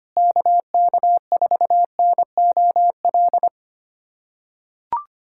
Morse code, heard as a keyed tone, about 25 wpm, 700 Hz. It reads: KK4NOL
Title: USA Callsign DE Callsign 25wpm